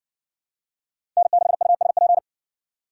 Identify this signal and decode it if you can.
Morse code, heard as a keyed tone, about 60 wpm, 700 Hz. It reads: N6USP